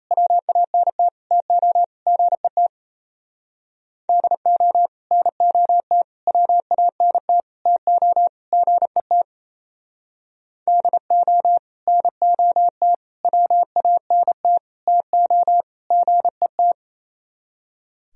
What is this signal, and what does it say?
Morse code, heard as a keyed tone, about 36 wpm, 700 Hz. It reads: WANT TO GET DO NOT WANT TO GET DO NOT WANT TO GET